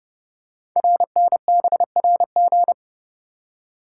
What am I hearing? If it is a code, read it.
Morse code, heard as a keyed tone, about 30 wpm, 700 Hz. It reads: RNBRG
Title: Code Groups - 5 Letters - 30wpm